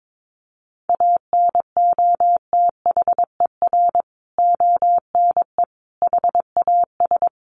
Morse code, heard as a keyed tone, about 22 wpm, 700 Hz. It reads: ANOTHER ONE HAS
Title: Additional Easy Phrases 22wpm